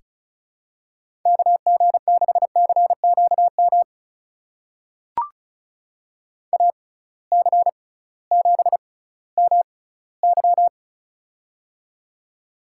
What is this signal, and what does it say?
Morse code, heard as a keyed tone, about 34 wpm, spaced as for 19 wpm, 700 Hz. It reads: KG6CQM AC7MY
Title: USA Call Signs - 6x Character Spacing - 35wpm